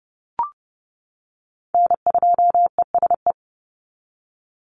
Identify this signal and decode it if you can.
Morse code, heard as a keyed tone, about 30 wpm, 700 Hz. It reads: N2ESE
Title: NA SOTA Chaser Callsigns 30wpm